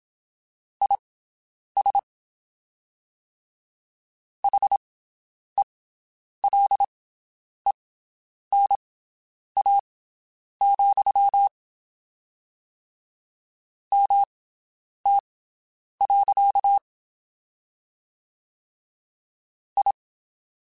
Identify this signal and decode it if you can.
Morse code, heard as a keyed tone, about 27 wpm, spaced as for 9 wpm, 800 Hz. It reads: IS HELENA, MT. I
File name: (spaced) / QSOs Spaced-mp3s / WD0BGP 1 @25 sp